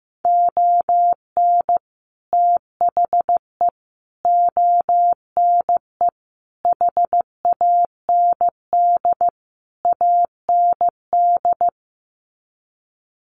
Morse code, heard as a keyed tone, about 15 wpm, 700 Hz. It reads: ON THE ONE HAND AND
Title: Sets of 5 Words Encore2 15wpm